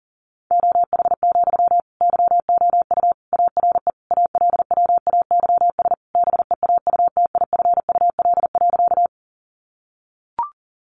Morse code, heard as a keyed tone, about 40 wpm, 700 Hz. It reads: OH, YOU ARE ALWAYS BEAUTIFUL.
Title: Sentences from Top 900 Words 40wpm